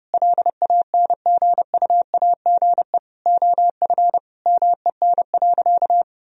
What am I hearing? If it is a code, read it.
Morse code, heard as a keyed tone, about 30 wpm, 700 Hz. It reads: LANGUAGE OF MEN.
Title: Sentences from Top 1000 Words - Review Part 3 - 30wpm